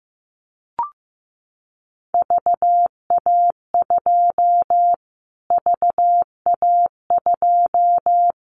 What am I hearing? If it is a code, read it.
Morse code, heard as a keyed tone, about 15 wpm, 700 Hz. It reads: VA2 VA2
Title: Callsign Prefixes for Top 25 DXCC - Repeated 3x - 15wpm